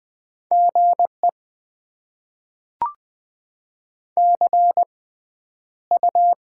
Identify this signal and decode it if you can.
Morse code, heard as a keyed tone, about 20 wpm, spaced as for 10 wpm, 700 Hz. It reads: GE CU
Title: Top 20 QSO Elements - 6x Character Spacing - 20wpm